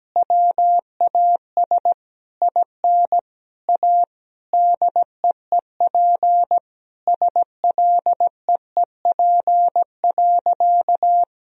Morse code, heard as a keyed tone, about 17 wpm, 700 Hz. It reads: WAS IN A DEEP SLEEP.
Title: Sentences from Top 1000 Words - Review Part 3 - 17wpm